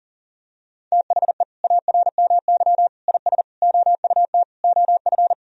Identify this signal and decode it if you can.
Morse code, heard as a keyed tone, about 40 wpm, 700 Hz. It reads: THE ARMY IS OUT OF
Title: Sentences from Top 1000 Words - Review Part 4 - 40wpm